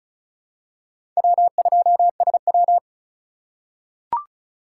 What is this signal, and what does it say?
Morse code, heard as a keyed tone, about 36 wpm, 700 Hz. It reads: W2SW
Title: USA Call Signs - 4 Characters - 4x Character Spacing - 35wpm